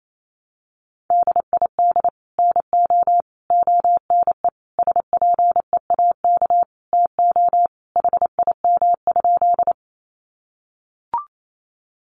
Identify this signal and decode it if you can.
Morse code, heard as a keyed tone, about 28 wpm, 700 Hz. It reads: DID NO ONE SPEAK TO HIM?
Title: Sentences from Top 1100 Words 28wpm